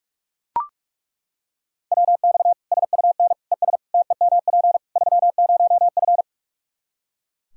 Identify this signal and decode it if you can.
Morse code, heard as a keyed tone, about 41 wpm, 700 Hz. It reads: WX SUN ES TEMP 30F
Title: Practice QSO - Exchange 2 - Info - 45wpm